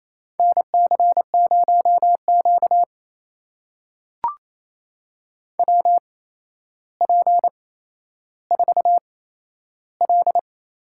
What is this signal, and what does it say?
Morse code, heard as a keyed tone, about 28 wpm, spaced as for 12 wpm, 700 Hz. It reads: NC0Q WP4L
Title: USA Call Signs - 4 Characters - 8x Character Spacing - 28wpm